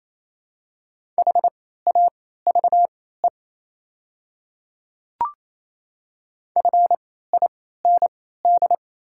Morse code, heard as a keyed tone, about 28 wpm, spaced as for 16 wpm, 690 Hz. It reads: HAVE FIND
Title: 3x Character Spacing - Top 100 Words - 28wpm